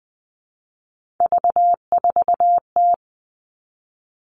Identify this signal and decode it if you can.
Morse code, heard as a keyed tone, about 20 wpm, 700 Hz. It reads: V4T
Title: All Cut Numbers - 3-Digit 20wpm